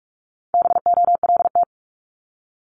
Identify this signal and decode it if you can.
Morse code, heard as a keyed tone, about 46 wpm, 700 Hz. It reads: BOLT
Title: Multi-letter Phonemes 45wpm